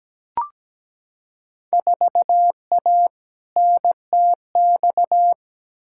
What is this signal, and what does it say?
Morse code, heard as a keyed tone, about 17 wpm, 700 Hz. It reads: 4A NTX
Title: Field Day Exchanges - 17wpm